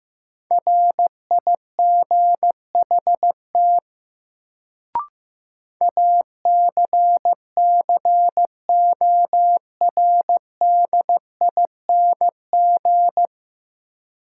Morse code, heard as a keyed tone, about 15 wpm, 700 Hz. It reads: RIGHT ACCORDING
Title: Sets of 2 Words - New Word Drill - 15wpm